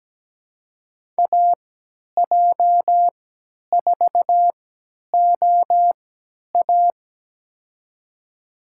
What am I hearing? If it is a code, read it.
Morse code, heard as a keyed tone, about 17 wpm, spaced as for 10 wpm, 700 Hz. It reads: AJ4OA
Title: USA Call Signs - 5 Characters - 3x Character Spacing - 17wpm